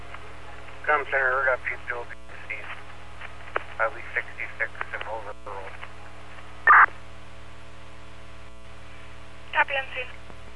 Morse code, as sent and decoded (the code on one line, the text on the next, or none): none